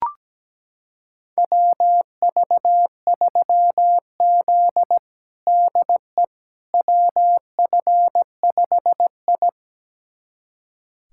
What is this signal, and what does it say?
Morse code, heard as a keyed tone, about 17 wpm, 700 Hz. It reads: WV3Z DE WF5I
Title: USA Callsign DE Callsign - 4 Characters - 17wpm